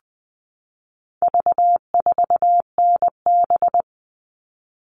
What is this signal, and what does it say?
Morse code, heard as a keyed tone, about 20 wpm, 700 Hz. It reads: V4NB
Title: All Cut Numbers - 4-Digit 20wpm